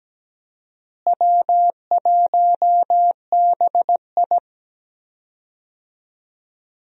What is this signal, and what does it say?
Morse code, heard as a keyed tone, about 17 wpm, 700 Hz. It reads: W1BI
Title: USA Call Signs - 4 Characters - Repeated 3x - 17wpm